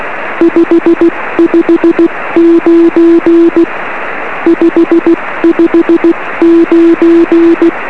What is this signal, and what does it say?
Morse code, heard as a keyed tone, about 16 wpm, 340 Hz. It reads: 559 559
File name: 1908z df1vb-p 13cm